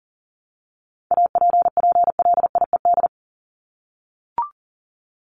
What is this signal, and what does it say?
Morse code, heard as a keyed tone, about 40 wpm, 700 Hz. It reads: APPLIED
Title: ED Suffix - Top 100 Words with Base Words 40wpm